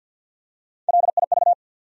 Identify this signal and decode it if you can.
Morse code, heard as a keyed tone, about 49 wpm, 700 Hz. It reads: LIV
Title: USA Common Names - Max 5 Letters - 50wpm